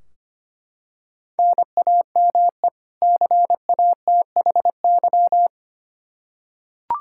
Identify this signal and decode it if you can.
Morse code, heard as a keyed tone, about 25 wpm, 700 Hz. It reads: NAME CATHY
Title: Practice QSO - Exchange 1 - Beginning - 25wpm